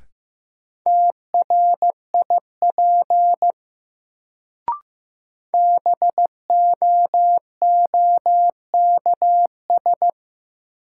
Morse code, heard as a keyed tone, about 15 wpm, 700 Hz. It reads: TRIP BOOKS
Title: Sentences from Top 1000 Words - Review Part 4 - New Word Drill - 15wpm